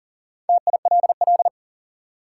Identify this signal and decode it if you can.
Morse code, heard as a keyed tone, about 37 wpm, 700 Hz. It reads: TILL
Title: Four Letter Words - 40wpm